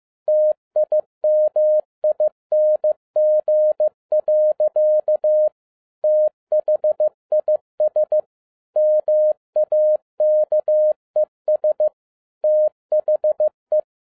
Morse code, heard as a keyed tone, about 15 wpm, 610 Hz. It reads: TIMING. THIS MAKES THE